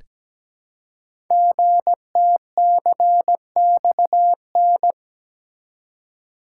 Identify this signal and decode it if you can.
Morse code, heard as a keyed tone, about 17 wpm, 700 Hz. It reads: GTCXN
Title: Code Groups - 5 Letters - 17wpm